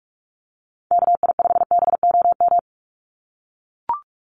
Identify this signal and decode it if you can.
Morse code, heard as a keyed tone, about 44 wpm, 700 Hz. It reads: KI5BOM
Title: USA Call Signs 45wpm